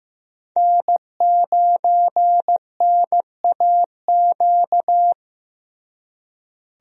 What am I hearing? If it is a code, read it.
Morse code, heard as a keyed tone, about 15 wpm, 700 Hz. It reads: N9NAQ
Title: USA Call Signs - 5 Characters - Repeated 3x - 15wpm